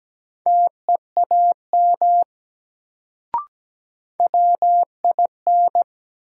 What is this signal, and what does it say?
Morse code, heard as a keyed tone, about 17 wpm, 700 Hz. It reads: TEAM WIN
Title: Sentences from Top 1500 Words - New Word Drill - 17wpm